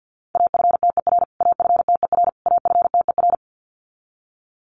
Morse code, heard as a keyed tone, about 50 wpm, 700 Hz. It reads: AFTER AFTER AFTER
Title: Top 300 Words Repeated 50wpm